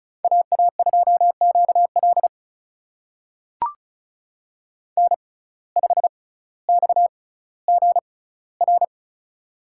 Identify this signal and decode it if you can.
Morse code, heard as a keyed tone, about 34 wpm, spaced as for 18 wpm, 700 Hz. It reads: AA2QL N5XGR
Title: USA Call Signs - 5 Characters - 6x Character Spacing - 35wpm